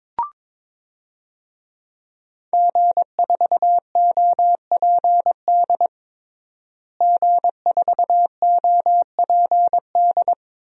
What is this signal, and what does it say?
Morse code, heard as a keyed tone, about 22 wpm, 700 Hz. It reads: G4OPD G4OPD
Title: DXCC-100 Call Signs - 5 Characters - Repeated 3x - 22wpm